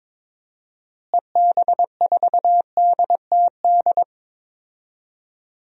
Morse code, heard as a keyed tone, about 22 wpm, 700 Hz. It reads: EB4DTD
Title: DXCC-100 Call Signs - 6 Characters - Repeated 3x - 22wpm